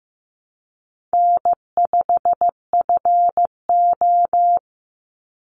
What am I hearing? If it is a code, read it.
Morse code, heard as a keyed tone, about 15 wpm, 700 Hz. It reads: N5FO
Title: Easy Call Signs - 15wpm